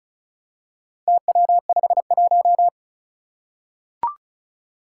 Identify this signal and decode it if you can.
Morse code, heard as a keyed tone, about 36 wpm, 700 Hz. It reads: TW51
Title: Four Challenging Letter-Number Combinations 35wpm